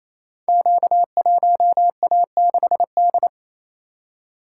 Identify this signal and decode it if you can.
Morse code, heard as a keyed tone, about 29 wpm, 700 Hz. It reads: Q1A6D
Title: Code Groups - 5 Letter:Numbers - 28wpm